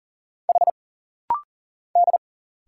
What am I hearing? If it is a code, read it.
Morse code, heard as a keyed tone, about 40 wpm, 700 Hz. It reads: H D
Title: Single Letters - Rapid-Fire 40wpm